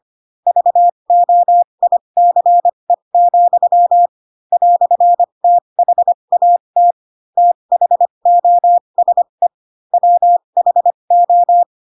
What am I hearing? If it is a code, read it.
Morse code, heard as a keyed tone, about 25 wpm, spaced as for 22 wpm, 700 Hz. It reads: VOICE, "THAT THOSE WHO
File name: Pinocchio / Pinocchio ch4 18 25 ik1hge